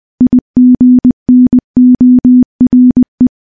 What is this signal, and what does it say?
Morse code, heard as a keyed tone, about 21 wpm, 250 Hz. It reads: IGNORE
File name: morse / morse-training-20WPM